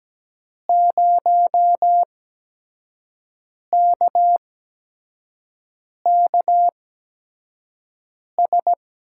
Morse code, heard as a keyed tone, about 17 wpm, spaced as for 5 wpm, 700 Hz. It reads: 0KKS
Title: USA Call Signs - 6 Characters - 8x Character Spacing - 17wpm